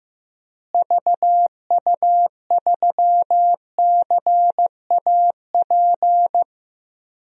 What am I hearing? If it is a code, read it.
Morse code, heard as a keyed tone, about 15 wpm, 700 Hz. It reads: VU3CAP
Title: DXCC-100 Call Signs - 6 Characters - Repeated 3x - 15wpm